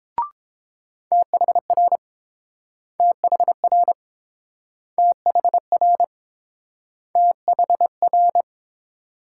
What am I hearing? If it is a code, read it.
Morse code, heard as a keyed tone, about 30 wpm, 700 Hz. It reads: THR THR THR THR